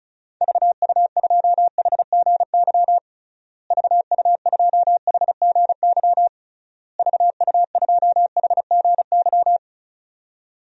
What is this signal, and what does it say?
Morse code, heard as a keyed tone, about 36 wpm, 700 Hz. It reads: VU2HGY VU2HGY VU2HGY
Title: DXCC-100 Call Signs - 6 Characters - Repeated 3x - 35wpm